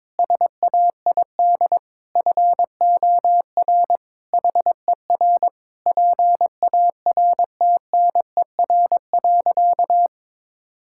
Morse code, heard as a keyed tone, about 22 wpm, 700 Hz. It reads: SAID FOR HER PARTNER.
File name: practice / 22wpm / Sentences with 8-Word Phrases - 22wpm